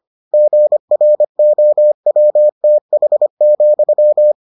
Morse code, heard as a keyed tone, about 25 wpm, 600 Hz. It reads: GROWTH,